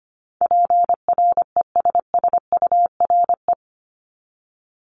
Morse code, heard as a keyed tone, about 25 wpm, 700 Hz. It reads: PRESSURE
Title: Multi-letter Phonemes Repeated 3x 25wpm